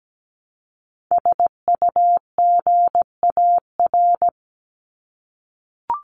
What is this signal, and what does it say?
Morse code, heard as a keyed tone, about 17 wpm, 700 Hz. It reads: SUGAR
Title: Top 1001-1250 Words 17wpm